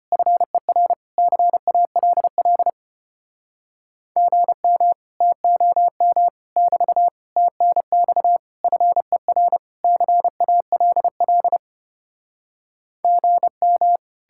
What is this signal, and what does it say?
Morse code, heard as a keyed tone, about 34 wpm, 700 Hz. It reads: FER CALL GM TOM <BT> TNX FER CALL GM